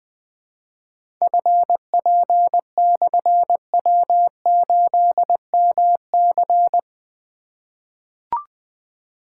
Morse code, heard as a keyed tone, about 20 wpm, 700 Hz. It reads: FP/W8MC
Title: Difficult Call Signs 20wpm